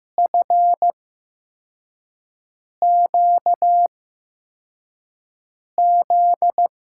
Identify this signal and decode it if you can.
Morse code, heard as a keyed tone, about 15 wpm, spaced as for 4 wpm, 700 Hz. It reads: FQZ
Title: USA Call Signs - 5 Characters - 8x Character Spacing - 15wpm